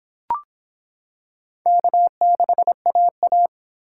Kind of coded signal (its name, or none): Morse code